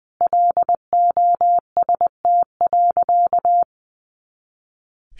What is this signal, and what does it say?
Morse code, heard as a keyed tone, about 20 wpm, 700 Hz. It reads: LOST.
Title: Additional Easy Phrases 20wpm